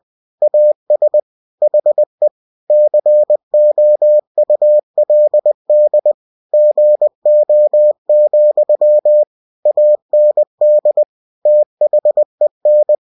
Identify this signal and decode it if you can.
Morse code, heard as a keyed tone, about 20 wpm, 600 Hz. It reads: AS HE COULD GO, AND THEN